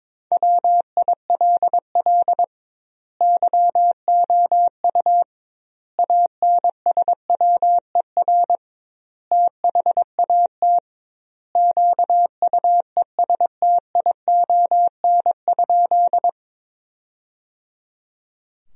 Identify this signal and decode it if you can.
Morse code, heard as a keyed tone, about 21 wpm, 700 Hz. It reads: WILL YOU ANSWER THAT QUESTION?
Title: 2x Word-Spacing - Sentences from Top 400 Words - 22wpm